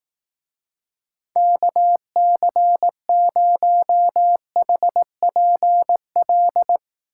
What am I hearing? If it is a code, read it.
Morse code, heard as a keyed tone, about 19 wpm, 700 Hz. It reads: KC0HPL
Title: Speed Racing - USA 2x3 Call Signs 15wpm